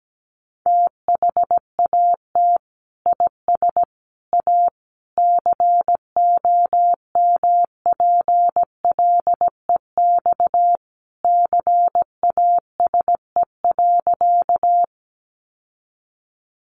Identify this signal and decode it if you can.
Morse code, heard as a keyed tone, about 17 wpm, 700 Hz. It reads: THAT IS A COMPLEX CASE.
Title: Sentences from Top 900 Words 17wpm